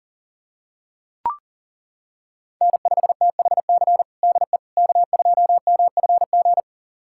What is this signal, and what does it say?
Morse code, heard as a keyed tone, about 40 wpm, 700 Hz. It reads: N5THC DE K2MFG
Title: USA Callsign DE Callsign - 5 Characters - 40wpm